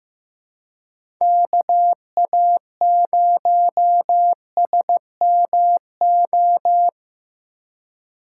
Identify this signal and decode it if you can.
Morse code, heard as a keyed tone, about 15 wpm, 700 Hz. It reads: KA0SMO
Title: USA Call Signs - 4x Character Spacing - 15wpm